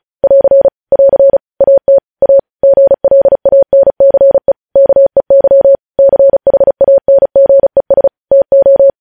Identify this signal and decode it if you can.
Morse code, heard as a keyed tone, about 36 wpm, 550 Hz. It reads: <AR> <AR> AT A GLANCE KEY CHANGES TO
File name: podcast / News headlines-1x-35x35